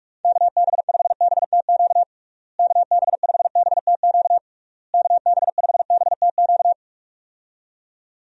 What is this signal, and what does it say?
Morse code, heard as a keyed tone, about 46 wpm, 700 Hz. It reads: KB5BTQ KB5BTQ KB5BTQ
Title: USA Call Signs - 6 Characters - Repeated 3x - 45wpm